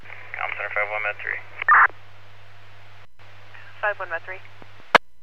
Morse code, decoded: none

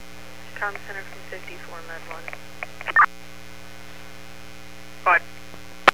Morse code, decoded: none

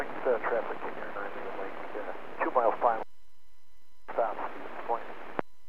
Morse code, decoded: none

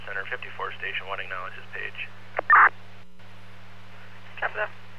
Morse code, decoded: none

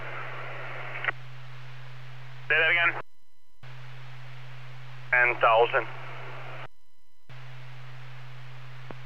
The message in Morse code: none

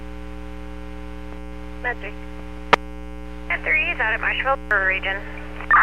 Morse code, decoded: none

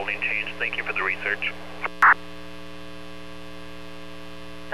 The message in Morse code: none

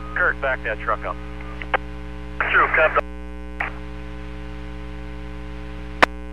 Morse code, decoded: none

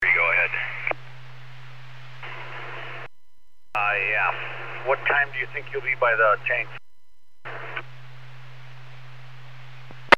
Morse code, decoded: none

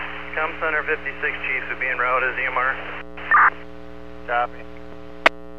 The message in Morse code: none

